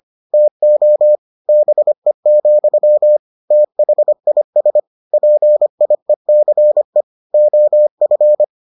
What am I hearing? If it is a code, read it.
Morse code, heard as a keyed tone, about 25 wpm, 600 Hz. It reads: TO BE, THIS PIECE OF